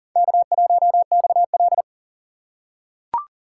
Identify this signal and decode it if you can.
Morse code, heard as a keyed tone, about 40 wpm, 700 Hz. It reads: K1XL
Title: USA Call Signs - 4 Characters - 4x Character Spacing - 40wpm